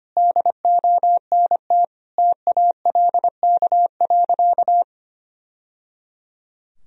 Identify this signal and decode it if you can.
Morse code, heard as a keyed tone, about 25 wpm, 700 Hz. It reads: DONT TALK.
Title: Sentences from Top 1000 Words - Review Part 2 - 25wpm